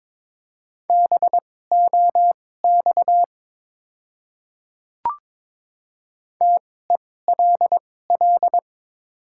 Morse code, heard as a keyed tone, about 22 wpm, spaced as for 16 wpm, 700 Hz. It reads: BOX TELL
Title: 2x Character Spacing - Top 500 Words - 22wpm